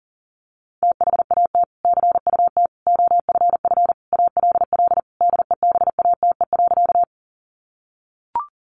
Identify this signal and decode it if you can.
Morse code, heard as a keyed tone, about 40 wpm, 700 Hz. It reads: THAT CUT OFF ALL DEBATE.
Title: Sentences from Top 1100 Words 40wpm